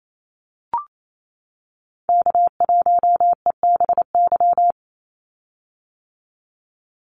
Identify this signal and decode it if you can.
Morse code, heard as a keyed tone, about 28 wpm, 700 Hz. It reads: K1EBY
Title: Call Signs - 28wpm